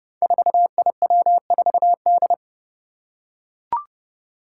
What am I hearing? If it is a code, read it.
Morse code, heard as a keyed tone, about 32 wpm, 700 Hz. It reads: 4IW4D
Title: Five Challenging Letter-Number Combinations 30wpm